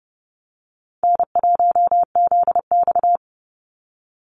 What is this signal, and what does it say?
Morse code, heard as a keyed tone, about 30 wpm, 700 Hz. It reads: N1ZX
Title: Easy Call Signs Repeated 2x - 30wpm